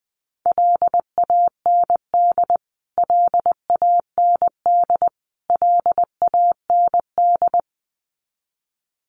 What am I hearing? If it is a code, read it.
Morse code, heard as a keyed tone, about 20 wpm, 700 Hz. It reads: LAND LAND LAND
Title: Top 500 Words Repeated 20wpm